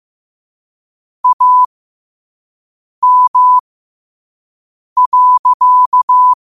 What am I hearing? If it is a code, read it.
Morse code, heard as a keyed tone, about 15 wpm, spaced as for 5 wpm, 1000 Hz. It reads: AM.
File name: qotdfiles / 5wpm